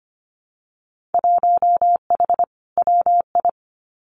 Morse code, heard as a keyed tone, about 25 wpm, 700 Hz. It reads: 1H WI
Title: Winter Field Day Exchanges 25wpm